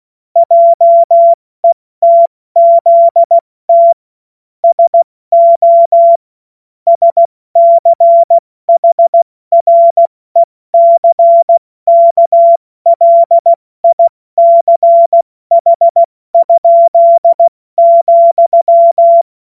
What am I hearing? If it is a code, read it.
Morse code, heard as a keyed tone, about 16 wpm, 680 Hz. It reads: JETZT SO SCHRECKLICH?,
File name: Morsekurs / Witze / Witz 20. Juni 2023 70 F80